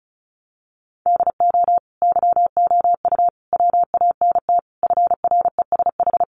Morse code, heard as a keyed tone, about 34 wpm, 700 Hz. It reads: DO YOU WANT FRESH
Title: Sentences from Top 1200 Words 35wpm